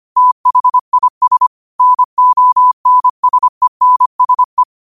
Morse code, heard as a keyed tone, about 25 wpm, 1000 Hz. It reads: THIS NONSENSE
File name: 25wpm